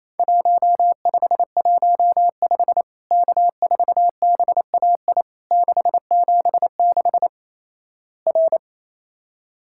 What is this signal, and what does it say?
Morse code, heard as a keyed tone, about 28 wpm, 700 Hz. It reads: 1515 K4BAI 676 R
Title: Worked All Europe DX Contest - QTC Exchange - 28wpm